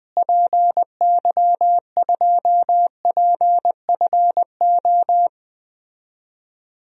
Morse code, heard as a keyed tone, about 20 wpm, 700 Hz. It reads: PY2PFO
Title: DXCC-100 Call Signs - 6 Characters - Repeated 2x - 20wpm